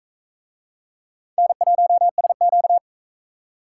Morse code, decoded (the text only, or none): N1SQ